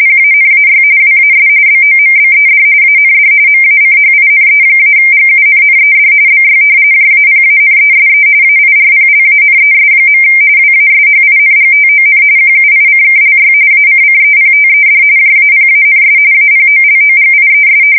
RTTY, about 45.45 baud. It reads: PANAY WAS BOMBED AND MACHINE-GUNNED BY JAPANESE AIRPLANES. WASHINGTON QUICKLY ACCEPTED JAPANS APOLOGIES
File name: fsk170 / usnavy4